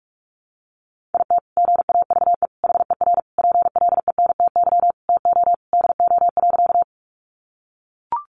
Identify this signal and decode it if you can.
Morse code, heard as a keyed tone, about 44 wpm, 700 Hz. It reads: IT GAVE HER PLENTY TO DO.
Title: Sentences from Top 1300 Words 45wpm